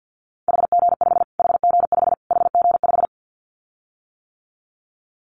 Morse code, heard as a keyed tone, about 50 wpm, 700 Hz. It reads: HG5 HG5 HG5
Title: Callsign Prefixes for Top 25 DXCC - Repeated 3x - 50wpm